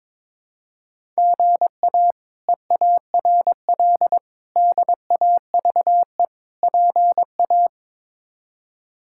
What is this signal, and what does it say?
Morse code, heard as a keyed tone, about 21 wpm, 700 Hz. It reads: GA EARL DAVE PA
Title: K1USN SST QSO Practice - 22wpm